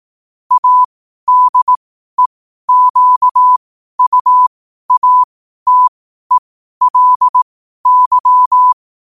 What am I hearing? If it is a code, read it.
Morse code, heard as a keyed tone, about 18 wpm, spaced as for 12 wpm, 1000 Hz. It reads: ADEQUATELY
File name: qotdfiles / 12wpm